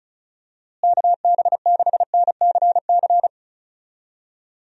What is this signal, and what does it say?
Morse code, heard as a keyed tone, about 34 wpm, 700 Hz. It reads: KB6NCC
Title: USA Call Signs - 6 Characters - Repeated 2x - 35wpm